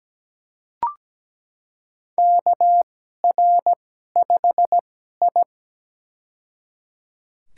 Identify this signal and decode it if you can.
Morse code, heard as a keyed tone, about 17 wpm, spaced as for 12 wpm, 700 Hz. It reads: KR5I
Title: USA Call Signs - 2x Character Spacing - 17wpm